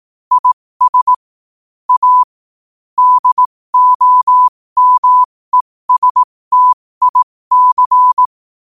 Morse code, heard as a keyed tone, about 18 wpm, spaced as for 15 wpm, 1000 Hz. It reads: IS A DOMESTIC
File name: qotdfiles / 15wpm